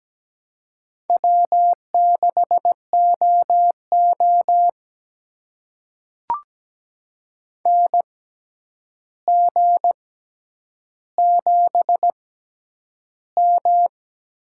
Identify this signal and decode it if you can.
Morse code, heard as a keyed tone, about 17 wpm, spaced as for 9 wpm, 700 Hz. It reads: W6OO NG7M
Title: USA Call Signs - 4 Characters - 6x Character Spacing - 17wpm